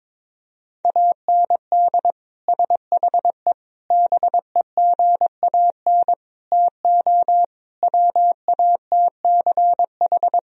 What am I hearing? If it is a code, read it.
Morse code, heard as a keyed tone, about 22 wpm, 700 Hz. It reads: AND SHE BEGAN TO WATCH
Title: Sentences from Top 1000 Words - Review Part 1 - 22wpm